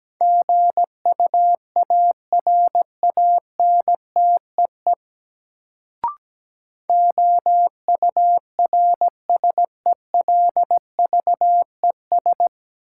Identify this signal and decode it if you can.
Morse code, heard as a keyed tone, about 17 wpm, 700 Hz. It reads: GUARANTEE OURSELVES
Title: Sentences from Top 1600 Words - New Word Drill Alt - 17wpm